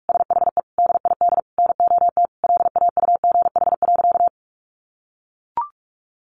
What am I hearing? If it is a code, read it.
Morse code, heard as a keyed tone, about 44 wpm, 700 Hz. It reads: SHE DID NOT LAUGH.
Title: Sentences from Top 1100 Words 45wpm